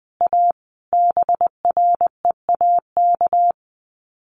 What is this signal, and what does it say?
Morse code, heard as a keyed tone, about 20 wpm, 700 Hz. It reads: A BREAK
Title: Sets of 5 Words Encore 20wpm